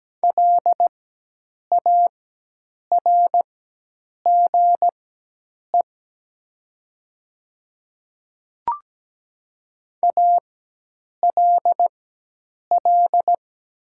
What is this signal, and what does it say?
Morse code, heard as a keyed tone, about 17 wpm, spaced as for 8 wpm, 700 Hz. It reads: LARGE ALL
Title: 4x Character Spacing - Top 500 Words - 17wpm